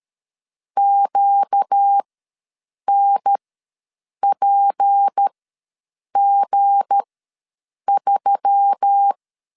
Morse code, heard as a keyed tone, about 13 wpm, spaced as for 7 wpm, 800 Hz. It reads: QNPG3
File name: LEZIONI / 7 GF9Z / 7 G ws 12wpm 3140